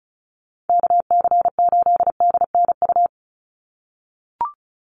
Morse code, heard as a keyed tone, about 34 wpm, 700 Hz. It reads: KC8DNU
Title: USA Call Signs 35wpm